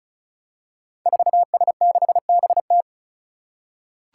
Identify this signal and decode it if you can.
Morse code, heard as a keyed tone, about 36 wpm, 700 Hz. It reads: 4S6BT
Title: Five Challenging Letter-Number Combinations 35wpm